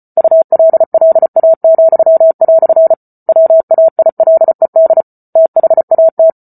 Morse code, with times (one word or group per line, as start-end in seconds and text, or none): 0.2-2.9 ULLA,"
3.3-5.0 WAILED
5.4-6.3 THAT